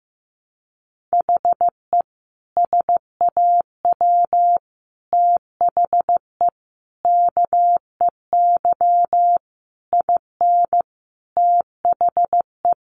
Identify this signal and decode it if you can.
Morse code, heard as a keyed tone, about 15 wpm, 700 Hz. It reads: HE SAW THE KEY IN THE
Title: Sentences from Top 1300 Words 15wpm